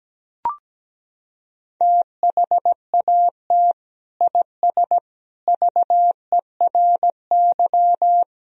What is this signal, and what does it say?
Morse code, heard as a keyed tone, about 17 wpm, 700 Hz. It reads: THAT IS VERY
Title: Sentences from Top 1000 Words - Review Part 2 - 17wpm